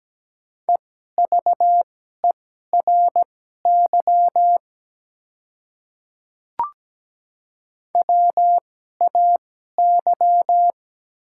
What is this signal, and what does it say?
Morse code, heard as a keyed tone, about 17 wpm, spaced as for 12 wpm, 700 Hz. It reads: EVERY WAY
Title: 2x Character Spacing - Top 200 Words - 17wpm